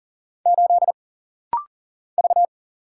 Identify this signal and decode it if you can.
Morse code, heard as a keyed tone, about 39 wpm, 700 Hz. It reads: 8 V
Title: Single Letter-Number - Rapid-Fire 40wpm